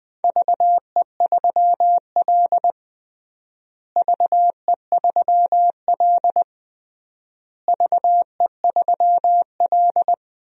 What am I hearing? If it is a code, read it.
Morse code, heard as a keyed tone, about 20 wpm, 700 Hz. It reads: VE3L VE3L VE3L
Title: DXCC-100 Call Signs - 4 Characters - Repeated 3x - 20wpm